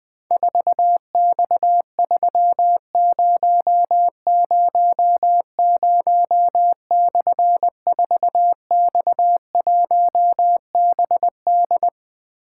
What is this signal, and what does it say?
Morse code, heard as a keyed tone, about 20 wpm, 700 Hz. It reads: 4X3000/4X1BD
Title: Difficult Call Signs 20wpm